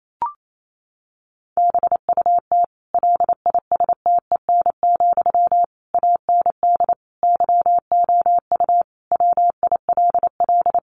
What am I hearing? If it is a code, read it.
Morse code, heard as a keyed tone, about 28 wpm, 700 Hz. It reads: BUT LISTEN, AND YOU WILL
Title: Sentences from Top 600 Words 28wpm